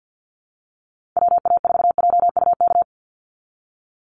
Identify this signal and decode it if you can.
Morse code, heard as a keyed tone, about 48 wpm, 700 Hz. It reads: WA4JUK
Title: Call Signs Repeated 3x - 50wpm